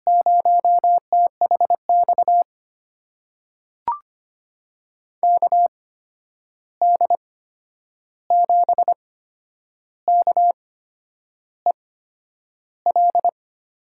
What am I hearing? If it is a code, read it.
Morse code, heard as a keyed tone, about 25 wpm, spaced as for 9 wpm, 700 Hz. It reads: 0THX KD7KEL
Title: USA Call Signs - 6 Characters - 8x Character Spacing - 25wpm